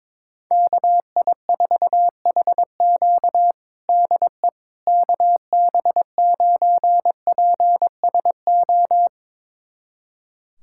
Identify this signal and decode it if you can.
Morse code, heard as a keyed tone, about 22 wpm, 700 Hz. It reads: KI4HQ DE KB9PSO
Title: USA Callsign DE Callsign 22wpm